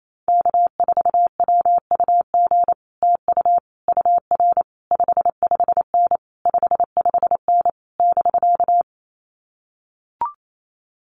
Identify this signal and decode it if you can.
Morse code, heard as a keyed tone, about 28 wpm, 700 Hz. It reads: K4WUG TU UR 55N 55N <BK>
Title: Common QSO Phrases Encore 28wpm